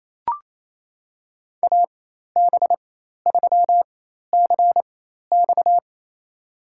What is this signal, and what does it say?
Morse code, heard as a keyed tone, about 27 wpm, spaced as for 13 wpm, 700 Hz. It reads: AB3CX
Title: USA Call Signs - 5 Characters - 4x Character Spacing - 28wpm